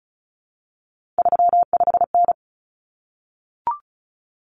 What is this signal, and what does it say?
Morse code, heard as a keyed tone, about 34 wpm, 700 Hz. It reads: 35N
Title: Common Cut Numbers - 3-Digit 35wpm